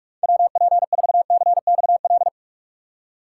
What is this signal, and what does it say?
Morse code, heard as a keyed tone, about 48 wpm, 700 Hz. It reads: WP4CXL
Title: USA Call Signs - 6 Characters - 4x Character Spacing - 45wpm